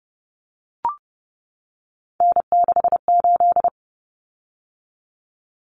Morse code, heard as a keyed tone, about 30 wpm, 700 Hz. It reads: N68
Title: Common Cut Numbers - 3-Digit 30wpm